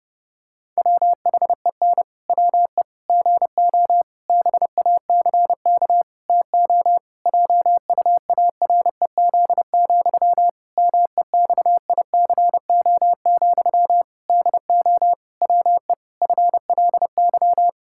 Morse code, heard as a keyed tone, about 32 wpm, 700 Hz. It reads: WHEN WE GO BACK TO JUAREZ, MEXICO, DO WE FLY
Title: Pangram Challenge v2 - 30wpm